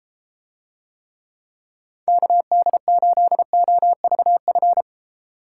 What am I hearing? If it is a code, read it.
Morse code, heard as a keyed tone, about 32 wpm, 700 Hz. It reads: KD8OVF